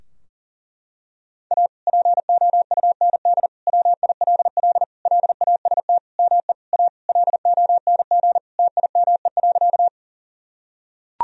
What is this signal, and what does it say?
Morse code, heard as a keyed tone, about 40 wpm, 700 Hz. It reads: A POUND WILL LAST ME A LONG TIME.
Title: Sentences from Top 1000 Words - Review Part 2 - 40wpm